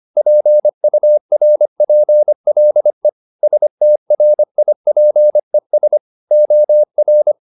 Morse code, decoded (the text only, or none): PURPLE STRIPES OR